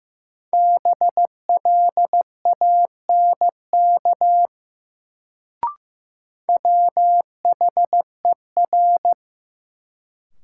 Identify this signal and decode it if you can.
Morse code, heard as a keyed tone, about 15 wpm, 700 Hz. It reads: BLANK WHER
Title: Sentences from Top 1500 Words - New Word Drill - 15wpm